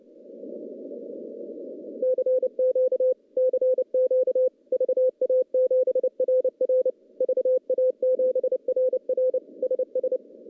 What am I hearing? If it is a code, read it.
Morse code, heard as a keyed tone, about 30 wpm, 500 Hz. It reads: CQ CQ VA7RR VA7RR SS
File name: contests / CW-SS-k5tr- -0900